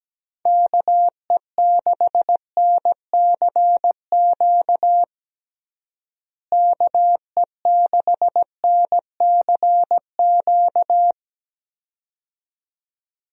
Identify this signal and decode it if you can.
Morse code, heard as a keyed tone, about 17 wpm, 700 Hz. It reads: KE6NCQ KE6NCQ
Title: USA Call Signs - 6 Characters - Repeated 2x - 17wpm